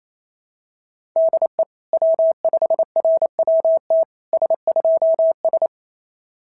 Morse code, heard as a keyed tone, about 25 wpm, 650 Hz. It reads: DE W5RWT S2S
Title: SOTA QSO Practice 28wpm